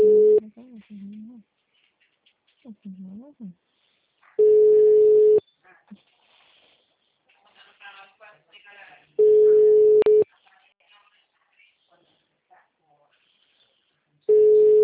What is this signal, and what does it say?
Ringback tone (Europe, 425 Hz), with speech.
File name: out- -2014- - - .73452